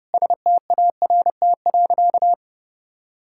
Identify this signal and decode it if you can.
Morse code, heard as a keyed tone, about 30 wpm, 700 Hz. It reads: START.
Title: Sentences from Top 1600 Words 30wpm